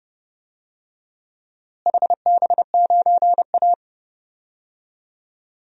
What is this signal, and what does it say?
Morse code, heard as a keyed tone, about 30 wpm, 700 Hz. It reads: HB9A
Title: DXCC-100 Call Signs - 4 Characters - Repeated 2x - 30wpm